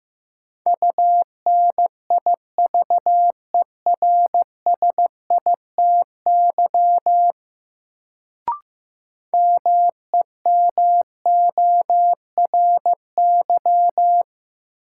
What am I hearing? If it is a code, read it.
Morse code, heard as a keyed tone, about 15 wpm, 700 Hz. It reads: UNIVERSITY MEMORY
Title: Sentences from Top 1300 Words - New Word Drill Alt - 15wpm